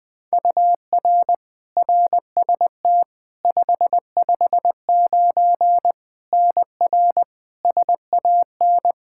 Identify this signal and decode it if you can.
Morse code, heard as a keyed tone, about 20 wpm, 700 Hz. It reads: UR RST 559 NR SAN
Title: K1USN SST QSO Practice 20wpm